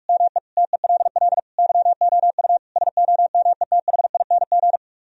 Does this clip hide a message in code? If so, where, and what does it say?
Yes: Morse code from 0.1 to 4.8 s, heard as a keyed tone, about 45 wpm, 700 Hz. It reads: ME TELL YOU SOMETHING